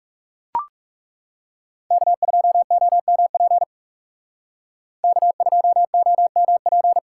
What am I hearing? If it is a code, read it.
Morse code, heard as a keyed tone, about 39 wpm, 700 Hz. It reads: K2OMP K2OMP